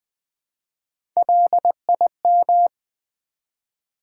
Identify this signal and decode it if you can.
Morse code, heard as a keyed tone, about 20 wpm, 700 Hz. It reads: LIM
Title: Build A Word - 1 Letter At A Time - 4-Letter Words - 20wpm